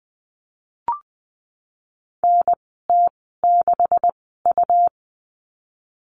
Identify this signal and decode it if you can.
Morse code, heard as a keyed tone, about 20 wpm, spaced as for 14 wpm, 700 Hz. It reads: NT6U
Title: 140 Lesson 23g - Letter f - Call Signs 20wpm